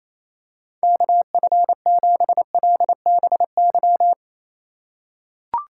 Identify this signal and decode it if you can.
Morse code, heard as a keyed tone, about 28 wpm, 700 Hz. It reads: KF7LBY
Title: USA Call Signs - 6 Characters - 4x Character Spacing - 28wpm